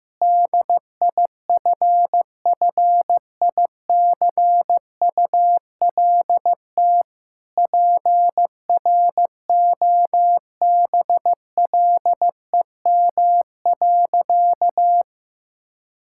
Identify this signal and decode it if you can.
Morse code, heard as a keyed tone, about 15 wpm, 700 Hz. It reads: DIFFICULT PROBLEM.
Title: Sentences from Top 1000 Words - Review Part 6 - 15wpm